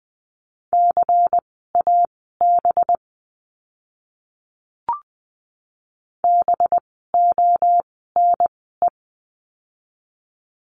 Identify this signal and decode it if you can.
Morse code, heard as a keyed tone, about 20 wpm, spaced as for 14 wpm, 700 Hz. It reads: CAB BONE
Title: 185 Lesson 30f - Letter b - Words 20wpm